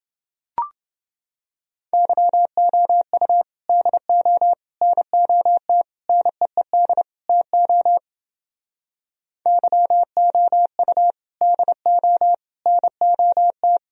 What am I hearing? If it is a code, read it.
Morse code, heard as a keyed tone, about 28 wpm, 700 Hz. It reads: YOU DO NOT NEED TO YOU DO NOT